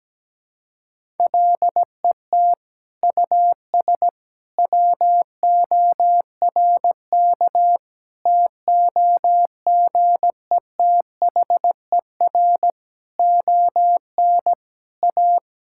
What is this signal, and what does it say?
Morse code, heard as a keyed tone, about 17 wpm, 700 Hz. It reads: LET US WORK TOGETHER ON A
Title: Sentences from Top 1000 Words - Review Part 5 - 17wpm